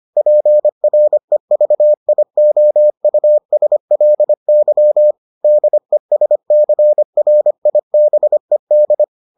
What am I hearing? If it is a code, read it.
Morse code, heard as a keyed tone, about 25 wpm, 600 Hz. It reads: PREVIOUSLY DESCRIBED